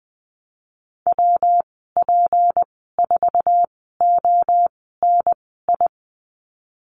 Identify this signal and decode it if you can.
Morse code, heard as a keyed tone, about 20 wpm, spaced as for 15 wpm, 700 Hz. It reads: WP4ONI
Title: 161 Lesson 26g - Letter g - Call Signs 20wpm